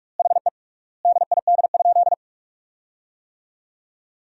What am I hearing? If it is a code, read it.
Morse code, heard as a keyed tone, about 46 wpm, 700 Hz. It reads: HE DID?
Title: 3x Word-Spacing - Sentences from Top 100 Words - 45wpm